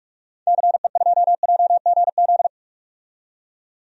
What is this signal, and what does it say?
Morse code, heard as a keyed tone, about 46 wpm, 700 Hz. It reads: CE2JGZ
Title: DXCC-100 Call Signs - 6 Characters - 45wpm